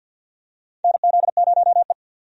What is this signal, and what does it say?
Morse code, heard as a keyed tone, about 52 wpm, 700 Hz. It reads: NZ0E
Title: USA Call Signs - 4 Characters - 50wpm